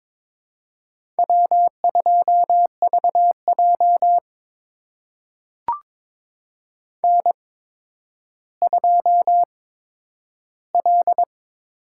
Morse code, heard as a keyed tone, about 21 wpm, spaced as for 11 wpm, 700 Hz. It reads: W2VJ N2L
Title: USA Call Signs - 4 Characters - 8x Character Spacing - 22wpm